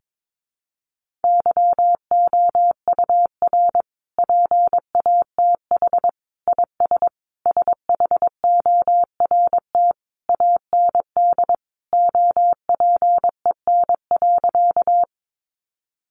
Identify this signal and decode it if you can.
Morse code, heard as a keyed tone, about 22 wpm, 700 Hz. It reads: YOUR PATH IS SHORT AND OPEN.
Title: Sentences from Top 1400 Words 22wpm